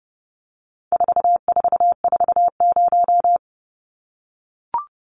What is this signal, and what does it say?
Morse code, heard as a keyed tone, about 30 wpm, 700 Hz. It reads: 4440
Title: Numbers - 4-Digit 30wpm